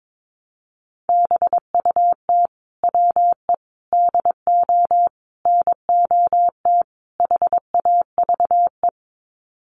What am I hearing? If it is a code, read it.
Morse code, heard as a keyed tone, about 22 wpm, 700 Hz. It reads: BUT WE DO NOT HAVE
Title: Sets of 5 Words Encore2 22wpm